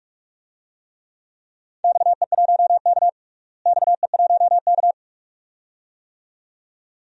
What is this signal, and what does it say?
Morse code, heard as a keyed tone, about 46 wpm, 700 Hz. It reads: XE1K XE1K
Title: DXCC-100 Call Signs - 4 Characters - Repeated 2x - 45wpm